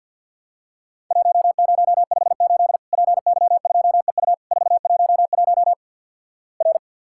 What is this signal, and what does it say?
Morse code, heard as a keyed tone, about 50 wpm, 700 Hz. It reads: 1058 PY2EU 411 R
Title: Worked All Europe DX Contest - QTC Exchange with Repeats - 50wpm